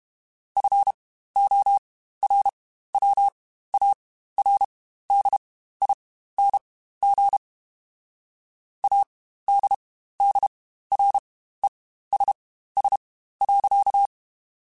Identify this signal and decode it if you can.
Morse code, heard as a keyed tone, about 32 wpm, spaced as for 15 wpm, 800 Hz. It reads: FORWARDING ADDRESS.
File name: Spaced / Chuckles Spaced-mp3s / Temptation sp @30